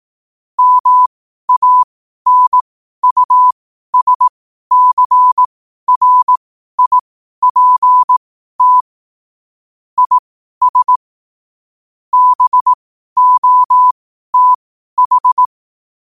Morse code, heard as a keyed tone, about 18 wpm, spaced as for 12 wpm, 1000 Hz. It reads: MANUSCRIPT IS BOTH
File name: qotdfiles / 12wpm